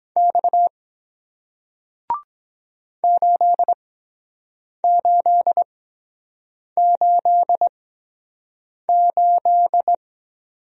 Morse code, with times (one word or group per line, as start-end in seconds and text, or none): 0.2-0.7 X
3.0-9.9 8888